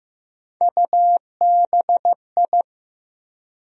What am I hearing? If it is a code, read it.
Morse code, heard as a keyed tone, about 15 wpm, 700 Hz. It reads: UBI
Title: USA Call Sign Postfixes - 3 Letters 15wpm